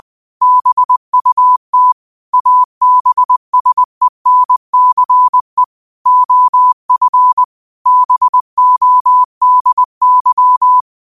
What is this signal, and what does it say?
Morse code, heard as a keyed tone, about 20 wpm, 1000 Hz. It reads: BUT ABSENCE OF BODY
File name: qotdfiles / 20wpm